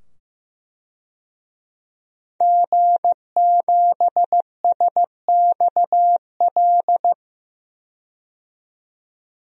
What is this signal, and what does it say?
Morse code, heard as a keyed tone, about 15 wpm, 700 Hz. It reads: G7SXL
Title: DXCC-100 Call Signs - 5 Characters - 15wpm